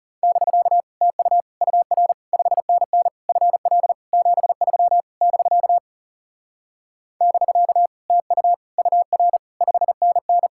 Morse code, heard as a keyed tone, about 40 wpm, 700 Hz. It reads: <BK> TU UR 5NN FL 73 <BK> <BK> TU UR 5NN